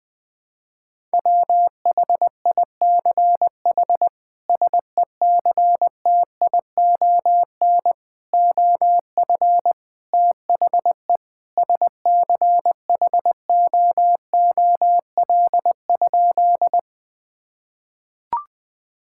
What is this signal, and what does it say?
Morse code, heard as a keyed tone, about 20 wpm, 700 Hz. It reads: WHICH SECTION OF THE SCHOOL?
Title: Sentences from Top 1000 Words - Review Part 5 - 20wpm